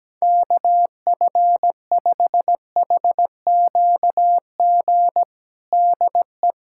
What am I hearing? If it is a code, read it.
Morse code, heard as a keyed tone, about 17 wpm, 700 Hz. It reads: KF5HQG DE
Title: USA Callsign DE Callsign 17wpm